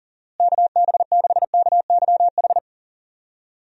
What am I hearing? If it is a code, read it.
Morse code, heard as a keyed tone, about 40 wpm, 700 Hz. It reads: KB6KYH
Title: USA Call Signs - 6 Characters - Repeated 2x - 40wpm